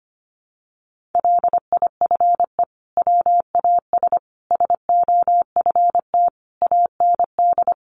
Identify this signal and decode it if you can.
Morse code, heard as a keyed tone, about 25 wpm, 700 Hz. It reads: LIFE WAS SOFT AND
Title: Sentences from Top 1500 Words 25wpm